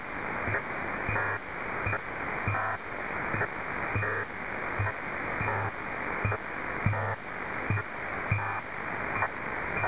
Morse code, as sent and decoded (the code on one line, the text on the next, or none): none